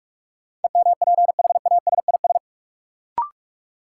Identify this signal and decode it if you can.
Morse code, heard as a keyed tone, about 46 wpm, 700 Hz. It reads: EMPHASIS
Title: Sentences from Top 1400 Words - New Word Drill Alt - 45wpm